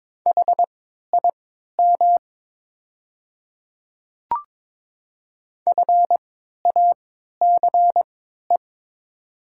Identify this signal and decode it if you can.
Morse code, heard as a keyed tone, about 22 wpm, spaced as for 13 wpm, 700 Hz. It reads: HIM FACE
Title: 3x Character Spacing - Top 300 Words - 22wpm